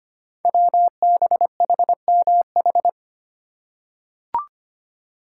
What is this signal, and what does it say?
Morse code, heard as a keyed tone, about 25 wpm, 700 Hz. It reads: WBHMH
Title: Code Groups - 5 Letter:Numbers - 25wpm